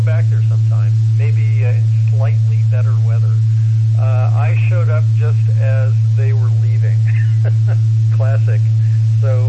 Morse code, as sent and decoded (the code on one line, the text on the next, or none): none